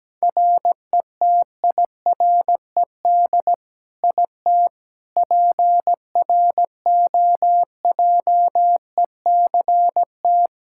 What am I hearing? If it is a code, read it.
Morse code, heard as a keyed tone, about 17 wpm, 700 Hz. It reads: RETIRED IT PROJECT
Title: Practice QSO - Exchange 3 - More Info - 17wpm